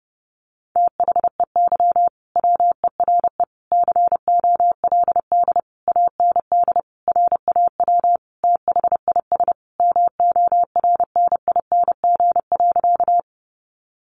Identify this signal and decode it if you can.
Morse code, heard as a keyed tone, about 30 wpm, 700 Hz. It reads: THEY WERE COLD AND RAW THIS MORNING.
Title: Sentences from Top 1500 Words 30wpm